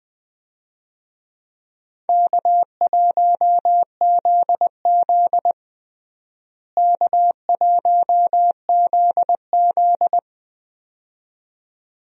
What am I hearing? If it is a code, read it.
Morse code, heard as a keyed tone, about 20 wpm, 700 Hz. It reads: K1ZZ K1ZZ
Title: USA Call Signs - 4 Characters - Repeated 2x - 20wpm